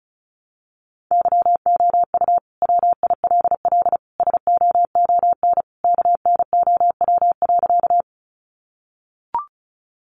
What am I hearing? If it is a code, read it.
Morse code, heard as a keyed tone, about 34 wpm, 700 Hz. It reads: YOU WILL SOON KNOW.
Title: Sentences from Top 400 Words 35wpm